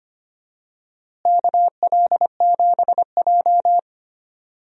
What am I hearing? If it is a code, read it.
Morse code, heard as a keyed tone, about 25 wpm, 700 Hz. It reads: KL7J
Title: DXCC-100 Call Signs - 4 Characters - Repeated 3x - 25wpm